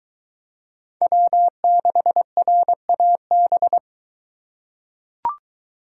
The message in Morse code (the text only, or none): W6RAB